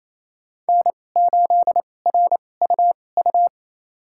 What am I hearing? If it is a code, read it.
Morse code, heard as a keyed tone, about 28 wpm, spaced as for 20 wpm, 700 Hz. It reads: N8RUU
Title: USA Call Signs - 5 Characters - 2x Character Spacing - 28wpm